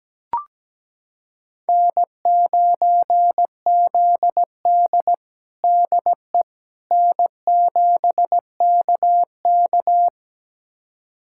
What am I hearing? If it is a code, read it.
Morse code, heard as a keyed tone, about 17 wpm, 700 Hz. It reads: N9ZD DE N7KK
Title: USA Callsign DE Callsign - 4 Characters - 17wpm